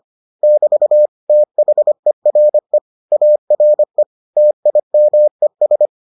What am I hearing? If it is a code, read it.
Morse code, heard as a keyed tone, about 25 wpm, 600 Hz. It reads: <BT> THERE ARE TIMES